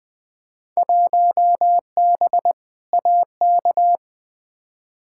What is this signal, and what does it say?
Morse code, heard as a keyed tone, about 21 wpm, 700 Hz. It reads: 1B AK
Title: Field Day Exchanges Encore - 20wpm